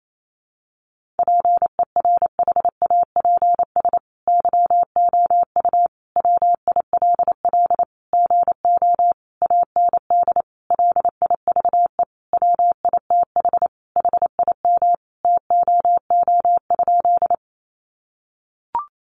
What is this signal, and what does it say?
Morse code, heard as a keyed tone, about 28 wpm, 700 Hz. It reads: PERHAPS YOU WILL GO AND LIVE WITH HIM TOO?
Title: Sentences from Top 500 Words 28wpm